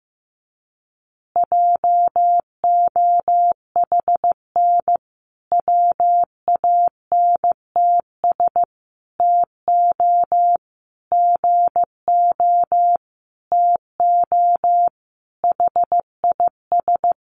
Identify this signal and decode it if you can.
Morse code, heard as a keyed tone, about 15 wpm, 700 Hz. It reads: JOHN WANTS TO GO TO HIS
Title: Additional Easy Phrases 15wpm